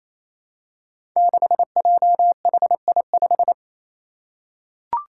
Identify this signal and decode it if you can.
Morse code, heard as a keyed tone, about 28 wpm, 700 Hz. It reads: 6JHI5
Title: Five Challenging Letter-Number Combinations 28wpm